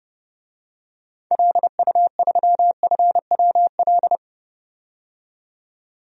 Morse code, heard as a keyed tone, about 32 wpm, 700 Hz. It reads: LU3FWL
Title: DXCC-100 Call Signs - 6 Characters - 30wpm